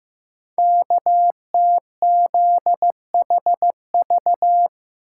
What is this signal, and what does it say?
Morse code, heard as a keyed tone, about 15 wpm, 700 Hz. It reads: KTZHV
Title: Code Groups - 5 Letters - 15wpm